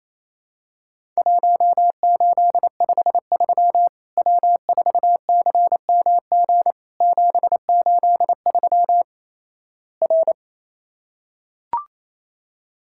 Morse code, heard as a keyed tone, about 28 wpm, 700 Hz. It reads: 1853 W4CMG 783 R
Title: Worked All Europe DX Contest - QTC Exchange - 28wpm